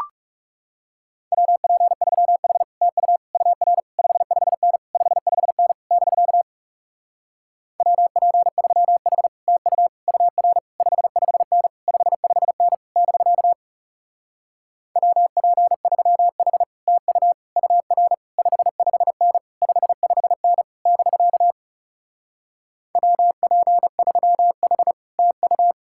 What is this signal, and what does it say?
Morse code, heard as a keyed tone, about 40 wpm, 700 Hz. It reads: WP3H TU UR 55N 55N <BK> WP3H TU UR 55N 55N <BK> WP3H TU UR 55N 55N <BK> WP3H TU